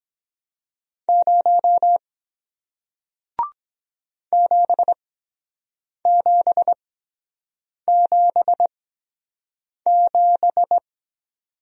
Morse code, heard as a keyed tone, about 23 wpm, spaced as for 7 wpm, 700 Hz. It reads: 0 7777